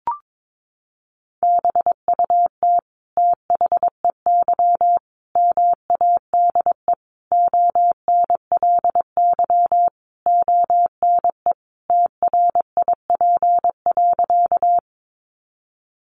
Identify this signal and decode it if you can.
Morse code, heard as a keyed tone, about 22 wpm, 700 Hz. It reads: BUT THEY MADE ONLY ONE TRIP.
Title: Sentences from Top 1300 Words 22wpm